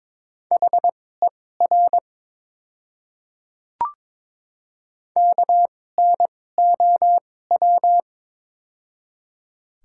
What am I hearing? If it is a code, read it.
Morse code, heard as a keyed tone, about 22 wpm, spaced as for 16 wpm, 700 Hz. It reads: HER KNOW
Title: 2x Character Spacing - Top 100 Words - 22wpm